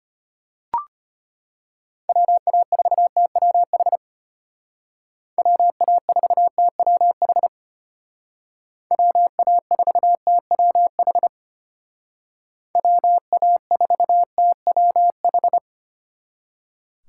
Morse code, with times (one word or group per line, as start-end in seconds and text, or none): 2.1-4.0 WA4TWH
5.4-7.5 WA4TWH
8.9-11.3 WA4TWH
12.8-15.6 WA4TWH